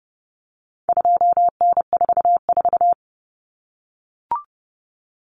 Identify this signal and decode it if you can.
Morse code, heard as a keyed tone, about 30 wpm, 700 Hz. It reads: 2N44
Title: Common Cut Numbers - 4-Digit 30wpm